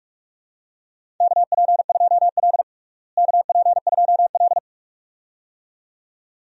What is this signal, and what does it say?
Morse code, heard as a keyed tone, about 46 wpm, 700 Hz. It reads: KP2L KP2L
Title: DXCC-100 Call Signs - 4 Characters - Repeated 2x - 45wpm